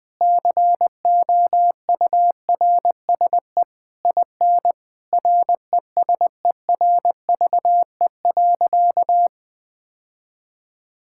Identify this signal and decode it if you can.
Morse code, heard as a keyed tone, about 20 wpm, 700 Hz. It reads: COURSE IN RESERVE.
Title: Sentences from Top 1000 Words - Review Part 6 - 20wpm